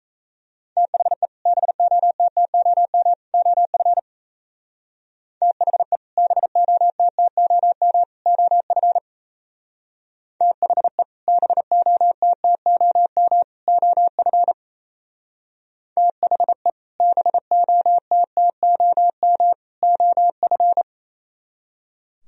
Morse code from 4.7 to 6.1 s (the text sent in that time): THE